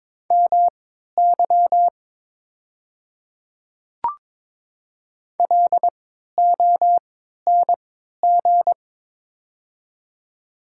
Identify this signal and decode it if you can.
Morse code, heard as a keyed tone, about 22 wpm, spaced as for 13 wpm, 690 Hz. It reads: MY LONG
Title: 3x Character Spacing - Top 200 Words - 22wpm